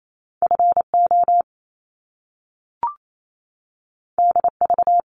Morse code, heard as a keyed tone, about 28 wpm, 700 Hz. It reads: FO DV
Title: Top Two Letter Combinations in Words 28wpm